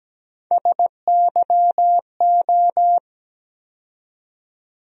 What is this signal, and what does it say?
Morse code, heard as a keyed tone, about 17 wpm, 700 Hz. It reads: SYO
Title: USA Call Sign Postfixes - 3 Letters 17wpm